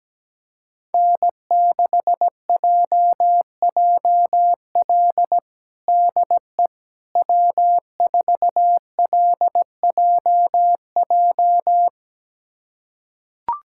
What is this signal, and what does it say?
Morse code, heard as a keyed tone, about 17 wpm, 700 Hz. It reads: N6JJL DE W4LJJ
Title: USA Callsign DE Callsign - 5 Characters - 17wpm